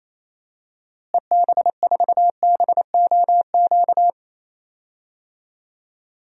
Morse code, heard as a keyed tone, about 27 wpm, 700 Hz. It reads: EB4BOQ
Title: DXCC-100 Call Signs - 6 Characters - Repeated 3x - 28wpm